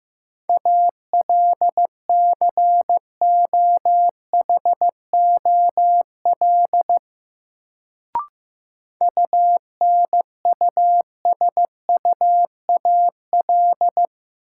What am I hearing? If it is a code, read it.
Morse code, heard as a keyed tone, about 15 wpm, 700 Hz. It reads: ALCOHOL UNUSUAL
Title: Sentences from Top 1400 Words - New Word Drill Alt - 15wpm